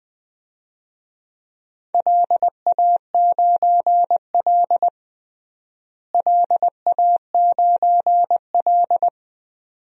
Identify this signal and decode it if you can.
Morse code, heard as a keyed tone, about 20 wpm, 700 Hz. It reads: LA9L LA9L
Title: DXCC-100 Call Signs - 4 Characters - Repeated 2x - 20wpm